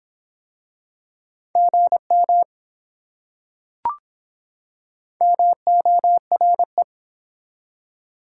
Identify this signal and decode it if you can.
Morse code, heard as a keyed tone, about 24 wpm, 700 Hz. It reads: GM MORE